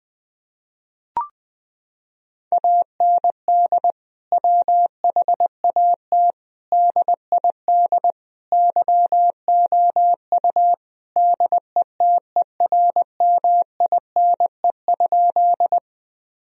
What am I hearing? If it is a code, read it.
Morse code, heard as a keyed tone, about 20 wpm, 700 Hz. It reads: AND WHAT DID YOU DETERMINE?
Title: Sentences from Top 1000 Words - Review Part 4 - 20wpm